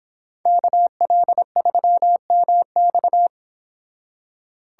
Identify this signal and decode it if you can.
Morse code, heard as a keyed tone, about 26 wpm, 700 Hz. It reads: KL3MX